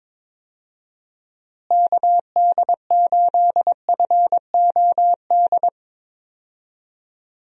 Morse code, heard as a keyed tone, about 21 wpm, 700 Hz. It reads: KD8FOD
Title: USA Call Signs - 6 Characters - Repeated 2x - 22wpm